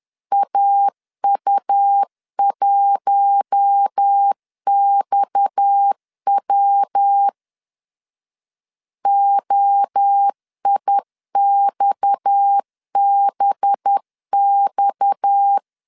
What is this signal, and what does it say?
Morse code, heard as a keyed tone, about 10 wpm, 800 Hz. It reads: AU1XW OIXBX
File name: LEZIONI / 6 PKX0 / 6 PKX 10wpm 2426